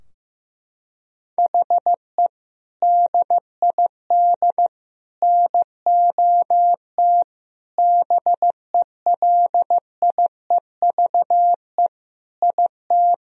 Morse code, heard as a keyed tone, about 15 wpm, 700 Hz. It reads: HE DID NOT BELIEVE IT
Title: Sentences from Top 1000 Words - Review Part 2 - 15wpm